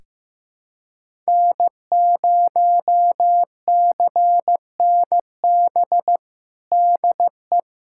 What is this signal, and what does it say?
Morse code, heard as a keyed tone, about 16 wpm, 700 Hz. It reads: N0CNB DE
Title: USA Callsign DE Callsign - 5 Characters - 15wpm